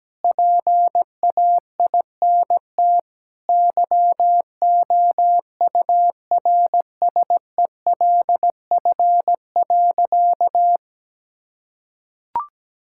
Morse code, heard as a keyed tone, about 17 wpm, 700 Hz. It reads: PAINT YOURSELF.
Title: Sentences from Top 1000 Words - Review Part 6 - 17wpm